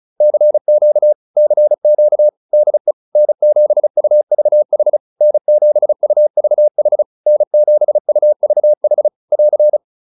Morse code, heard as a keyed tone, about 34 wpm, 600 Hz. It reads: CQ CQ DE N7UVH N7UVH N7UVH <AR>
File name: QSO105 35